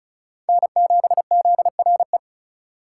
Morse code, heard as a keyed tone, about 36 wpm, 700 Hz. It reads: N7ZRE
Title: USA Call Signs - 5 Characters - 4x Character Spacing - 35wpm